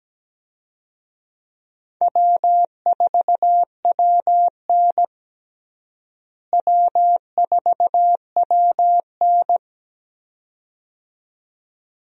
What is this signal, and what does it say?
Morse code, heard as a keyed tone, about 17 wpm, 700 Hz. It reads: W4WN W4WN
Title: USA Call Signs - 4 Characters - Repeated 2x - 17wpm